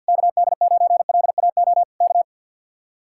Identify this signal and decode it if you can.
Morse code, heard as a keyed tone, about 65 wpm, 700 Hz. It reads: KD9LAO K